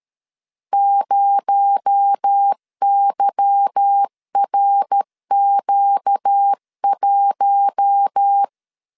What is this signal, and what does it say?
Morse code, heard as a keyed tone, about 13 wpm, 800 Hz. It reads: 0YRQ1
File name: LEZIONI / 8 YJ45 / 8 Y 12wpm 2228